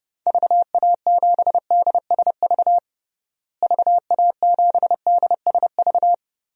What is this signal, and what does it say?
Morse code, heard as a keyed tone, about 30 wpm, 700 Hz. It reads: VA7DSV VA7DSV
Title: DXCC-100 Call Signs - 6 Characters - Repeated 3x - 30wpm